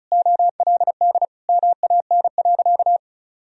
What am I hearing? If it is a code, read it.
Morse code, heard as a keyed tone, about 36 wpm, 700 Hz. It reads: OLD MAN.
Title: Sentences from Top 1000 Words - Review Part 2 - 35wpm